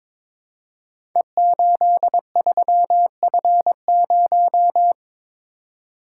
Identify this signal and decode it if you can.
Morse code, heard as a keyed tone, about 22 wpm, 700 Hz. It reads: E83F0
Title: Code Groups - 5 Letter:Numbers - 22wpm